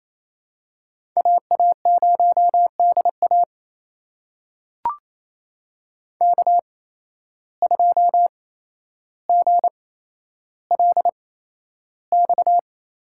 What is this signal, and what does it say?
Morse code, heard as a keyed tone, about 28 wpm, spaced as for 12 wpm, 700 Hz. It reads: AA0DA K2GLX
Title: USA Call Signs - 5 Characters - 8x Character Spacing - 28wpm